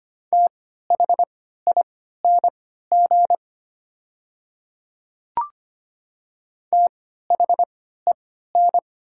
Morse code, heard as a keyed tone, about 25 wpm, spaced as for 14 wpm, 690 Hz. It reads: THING THEN
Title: 3x Character Spacing - Top 200 Words - 25wpm